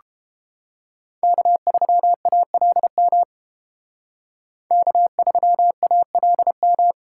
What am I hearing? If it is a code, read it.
Morse code, heard as a keyed tone, about 29 wpm, 700 Hz. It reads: K3ALM K3ALM